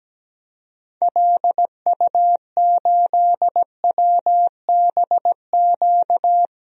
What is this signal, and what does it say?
Morse code, heard as a keyed tone, about 17 wpm, 700 Hz. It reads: LU8WBQ
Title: DXCC-100 Call Signs - 6 Characters - 17wpm